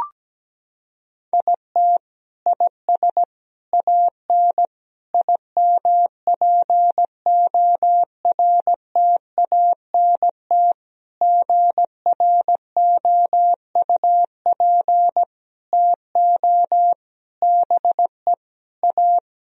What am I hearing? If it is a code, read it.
Morse code, heard as a keyed tone, about 17 wpm, 700 Hz. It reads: IT IS AN IMPORTANT GROUP TO BE A
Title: Sentences from Top 1000 Words - Review Part 1 - 17wpm